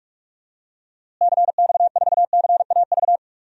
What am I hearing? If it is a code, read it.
Morse code, heard as a keyed tone, about 41 wpm, 700 Hz. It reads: CX4CAV
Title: DXCC-100 Call Signs - 6 Characters - Repeated 3x - 45wpm